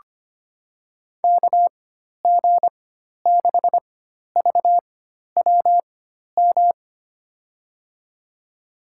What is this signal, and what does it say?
Morse code, heard as a keyed tone, about 25 wpm, spaced as for 12 wpm, 700 Hz. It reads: KG6VWM
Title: USA Call Signs - 6 Characters - 4x Character Spacing - 25wpm